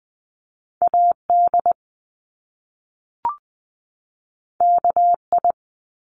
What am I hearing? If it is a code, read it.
Morse code, heard as a keyed tone, about 20 wpm, 700 Hz. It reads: AD KI
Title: Top Two Letter Combinations in Words 20wpm